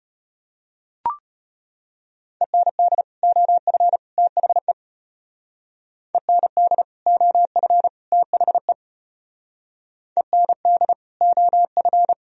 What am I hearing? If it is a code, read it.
Morse code, heard as a keyed tone, about 32 wpm, 700 Hz. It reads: END OF THE END OF THE END OF